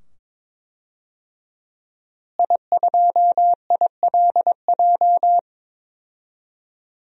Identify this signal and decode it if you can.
Morse code, heard as a keyed tone, about 21 wpm, 700 Hz. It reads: I2ILJ
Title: DXCC-100 Call Signs - 5 Characters - Repeated 2x - 22wpm